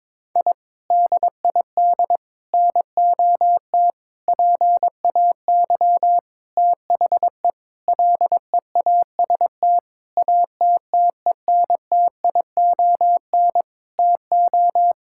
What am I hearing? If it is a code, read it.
Morse code, heard as a keyed tone, about 22 wpm, 700 Hz. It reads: I DID NOT PAY THE LEAST ATTENTION TO
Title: Sentences from Top 1000 Words - Review Part 6 - 22wpm